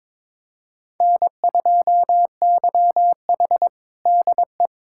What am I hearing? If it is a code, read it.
Morse code, heard as a keyed tone, about 22 wpm, 700 Hz. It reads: N2YH DE
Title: USA Callsign DE Callsign - 4 Characters - 22wpm